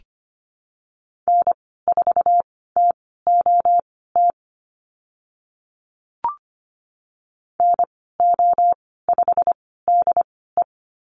Morse code, heard as a keyed tone, about 25 wpm, spaced as for 16 wpm, 690 Hz. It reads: N4TOT NO5DE
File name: Lesson 15g - Number 5 - Call Signs 25wpm